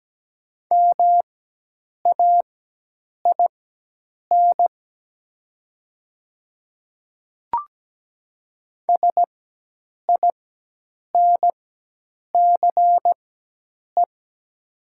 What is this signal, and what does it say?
Morse code, heard as a keyed tone, about 17 wpm, spaced as for 8 wpm, 700 Hz. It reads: MAIN SINCE
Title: 4x Character Spacing - Top 300 Words - 17wpm